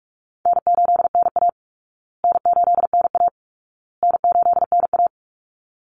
Morse code, heard as a keyed tone, about 44 wpm, 700 Hz. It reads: N8NA N8NA N8NA
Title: Easy Call Signs Repeated 3x - 45wpm